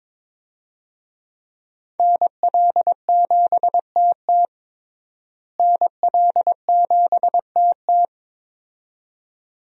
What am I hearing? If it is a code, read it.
Morse code, heard as a keyed tone, about 21 wpm, 700 Hz. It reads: NL7TT NL7TT
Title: USA Call Signs - 5 Characters - Repeated 2x - 22wpm